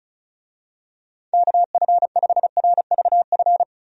Morse code, heard as a keyed tone, about 36 wpm, 700 Hz. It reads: KF5RVF
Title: USA Call Signs - 6 Characters - 35wpm